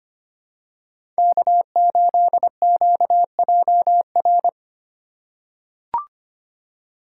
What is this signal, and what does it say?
Morse code, heard as a keyed tone, about 25 wpm, 700 Hz. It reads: K8QJR
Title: USA Call Signs - 5 Characters - 3x Character Spacing - 25wpm